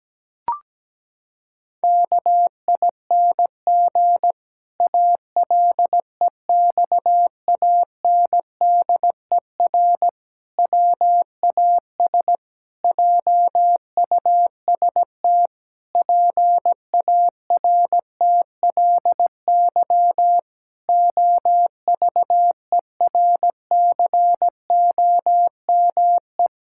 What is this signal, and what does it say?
Morse code, heard as a keyed tone, about 17 wpm, 700 Hz. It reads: KING ALEXANDER WAS JUST PARTLY OVERCOME
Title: Pangram Challenge v2 - 17wpm